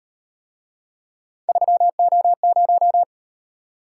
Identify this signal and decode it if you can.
Morse code, heard as a keyed tone, about 36 wpm, 700 Hz. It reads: 3O0